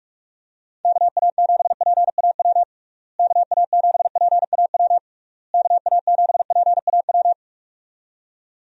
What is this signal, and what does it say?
Morse code, heard as a keyed tone, about 46 wpm, 700 Hz. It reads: KA7PAW KA7PAW KA7PAW
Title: USA Call Signs - 6 Characters - Repeated 3x - 45wpm